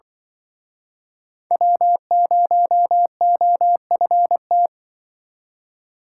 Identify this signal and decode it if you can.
Morse code, heard as a keyed tone, about 24 wpm, 700 Hz. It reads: W0OFT
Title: Speed Racing - USA 1x3 Call Signs 20wpm